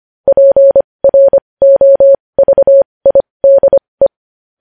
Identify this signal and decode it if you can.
Morse code, heard as a keyed tone, about 25 wpm, spaced as for 20 wpm, 550 Hz. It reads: PROVIDE
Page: News Headlines in Morse Code at 20 WPM